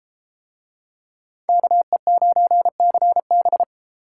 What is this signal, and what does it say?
Morse code, heard as a keyed tone, about 30 wpm, 700 Hz. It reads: KE9CB